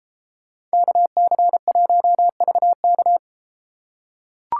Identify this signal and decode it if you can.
Morse code, heard as a keyed tone, about 32 wpm, 700 Hz. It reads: KC1VK